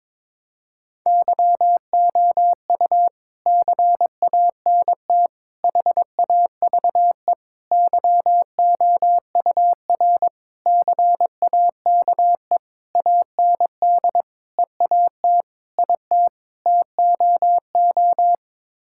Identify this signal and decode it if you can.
Morse code, heard as a keyed tone, about 22 wpm, 700 Hz. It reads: YOU CANT HAVE YOUR CAKE AND EAT IT TOO
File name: Sets of 9 Words - 22wpm